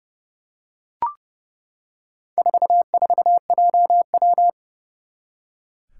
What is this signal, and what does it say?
Morse code, heard as a keyed tone, about 29 wpm, 690 Hz. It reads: 44JW
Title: Four Challenging Letter-Number Combinations 30wpm